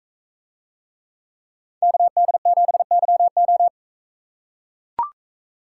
Morse code, heard as a keyed tone, about 40 wpm, 700 Hz. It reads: KD7YO